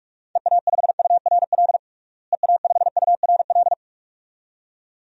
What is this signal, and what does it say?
Morse code, heard as a keyed tone, about 46 wpm, 700 Hz. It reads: EA5URL EA5URL
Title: DXCC-100 Call Signs - 6 Characters - Repeated 2x - 45wpm